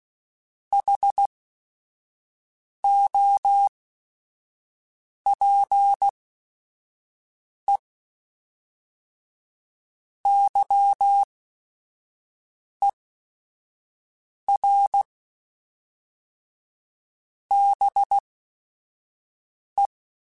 Morse code, heard as a keyed tone, about 16 wpm, spaced as for 5 wpm, 800 Hz. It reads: HOPE YER BE